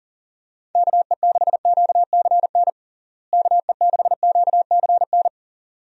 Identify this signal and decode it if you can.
Morse code, heard as a keyed tone, about 37 wpm, 700 Hz. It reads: KE6QCN KE6QCN
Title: USA Call Signs - 6 Characters - Repeated 2x - 40wpm